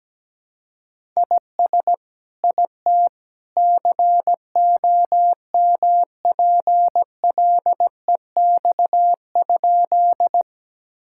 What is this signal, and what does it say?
Morse code, heard as a keyed tone, about 17 wpm, 700 Hz. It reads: IS IT COMPLEX?
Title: Sentences from Top 1000 Words - Review Part 6 - 17wpm